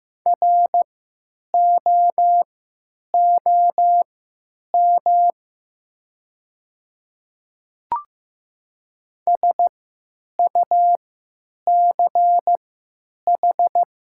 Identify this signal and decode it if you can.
Morse code, heard as a keyed tone, about 15 wpm, spaced as for 9 wpm, 700 Hz. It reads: ROOM SUCH
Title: 3x Character Spacing - Top 300 Words - 15wpm